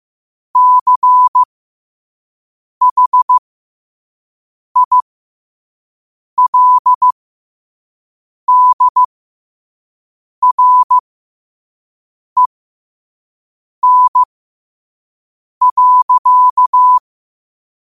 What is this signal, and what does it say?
Morse code, heard as a keyed tone, about 15 wpm, spaced as for 5 wpm, 1000 Hz. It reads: CHILDREN.
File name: qotdfiles / 5wpm